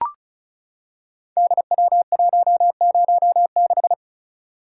Morse code, heard as a keyed tone, about 36 wpm, 690 Hz. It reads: DW106
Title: Code Groups - 5 Letter:Numbers - 35wpm